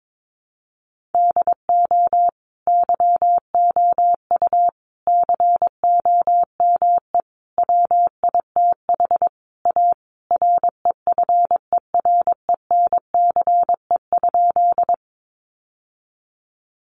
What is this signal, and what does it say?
Morse code, heard as a keyed tone, about 22 wpm, 700 Hz. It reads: DO YOU COME WITH A REFERENCE?
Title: Sentences from Top 1200 Words 22wpm